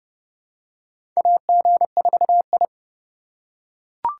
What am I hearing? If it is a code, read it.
Morse code, heard as a keyed tone, about 30 wpm, 700 Hz. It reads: AG4I
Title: USA Call Signs - 3x Character Spacing - 30wpm